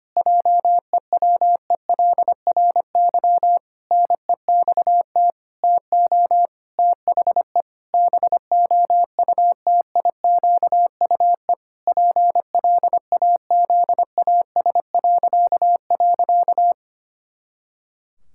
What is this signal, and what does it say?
Morse code, heard as a keyed tone, about 25 wpm, 700 Hz. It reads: JEWELRY NEXT TO THE BOUTIQUE PLAZAS..
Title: Pangram Challenge v2 - 25wpm